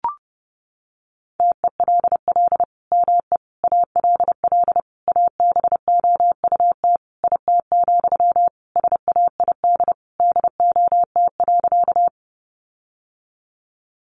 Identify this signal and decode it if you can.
Morse code, heard as a keyed tone, about 30 wpm, 700 Hz. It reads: TELL ME ALL ABOUT IT, SAID DOT.
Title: Sentences from Top 1500 Words 30wpm